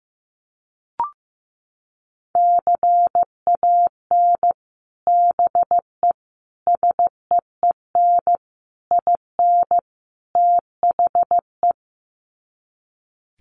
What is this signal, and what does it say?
Morse code, heard as a keyed tone, about 15 wpm, 700 Hz. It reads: CAN BE SEEN IN THE
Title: Sets of 5 Words Encore2 15wpm